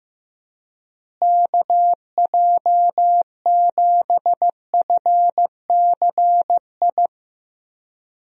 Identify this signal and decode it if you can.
Morse code, heard as a keyed tone, about 15 wpm, 700 Hz. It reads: KJ7FCI
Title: USA Call Signs - 6 Characters - Repeated 3x - 15wpm